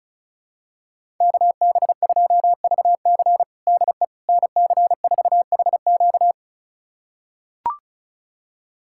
Morse code, heard as a keyed tone, about 36 wpm, 700 Hz. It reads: KB2VC DE NC4HQ
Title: USA Callsign DE Callsign - 5 Characters - 35wpm